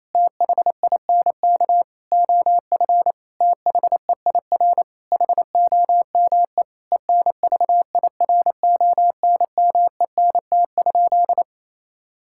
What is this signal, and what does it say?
Morse code, heard as a keyed tone, about 28 wpm, 700 Hz. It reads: THINK OF THEIR HOME ENVIRONMENT?
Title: Sentences from Top 1000 Words - Review Part 5 - 28wpm